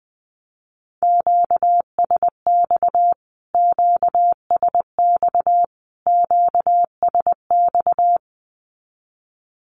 Morse code, heard as a keyed tone, about 19 wpm, 700 Hz. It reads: QSX QSX QSX
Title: Q-Codes Repeated 3x 20wpm